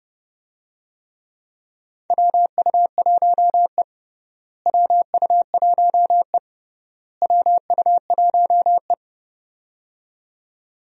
Morse code, heard as a keyed tone, about 32 wpm, 700 Hz. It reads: WU1E WU1E WU1E
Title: USA Call Signs - 4 Characters - Repeated 3x - 30wpm